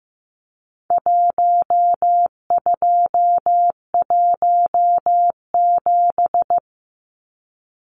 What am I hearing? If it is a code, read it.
Morse code, heard as a keyed tone, about 15 wpm, 700 Hz. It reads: 1217
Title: Numbers - Contest Style - 4-Digit 15wpm